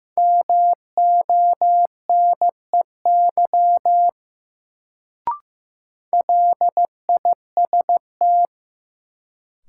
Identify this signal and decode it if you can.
Morse code, heard as a keyed tone, about 15 wpm, 700 Hz. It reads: MONEY LIST
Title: Sentences from Top 400 Words - New Word Drill - 15wpm